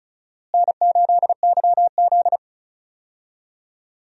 Morse code, heard as a keyed tone, about 36 wpm, 700 Hz. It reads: N8YZ
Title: USA Call Signs - 4 Characters - Repeated 3x - 35wpm